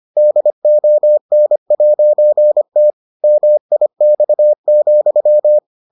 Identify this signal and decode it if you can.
Morse code, heard as a keyed tone, about 25 wpm, 600 Hz. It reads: DON'T MIX,